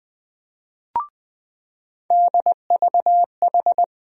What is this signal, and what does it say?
Morse code, heard as a keyed tone, about 21 wpm, 700 Hz. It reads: DVH
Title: USA Call Sign Postfixes - 3 Letters 20wpm